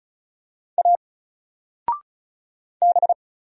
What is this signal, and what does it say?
Morse code, heard as a keyed tone, about 36 wpm, 700 Hz. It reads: A B
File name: Single-Letter-Number-35wpm